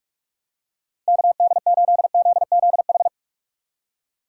Morse code, heard as a keyed tone, about 46 wpm, 700 Hz. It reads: KD8ZZH
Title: USA Call Signs - 6 Characters - 6x Character Spacing - 45wpm